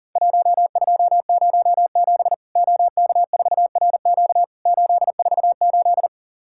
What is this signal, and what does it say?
Morse code, heard as a keyed tone, about 40 wpm, 700 Hz. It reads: 1207 OK4RQ 848
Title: Worked All Europe DX Contest - QTC Exchange - 40wpm